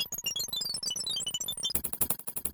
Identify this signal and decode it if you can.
There is no code.